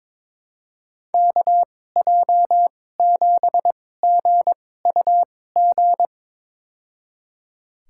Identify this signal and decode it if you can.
Morse code, heard as a keyed tone, about 22 wpm, spaced as for 16 wpm, 700 Hz. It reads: KJ7GUG
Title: USA Call Signs - 6 Characters - 2x Character Spacing - 22wpm